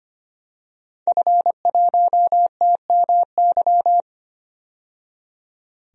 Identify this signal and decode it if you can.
Morse code, heard as a keyed tone, about 25 wpm, 700 Hz. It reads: F1TMY
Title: DXCC-100 Call Signs - 5 Characters - 25wpm